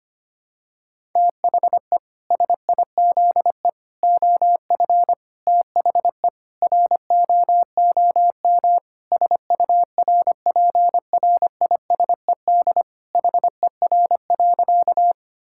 Morse code, heard as a keyed tone, about 25 wpm, 700 Hz. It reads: THE SIZE OF THE ROOM SURPRISED HER.
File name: practice / 25wpm / Sentences with 4-Word Phrases - 25wpm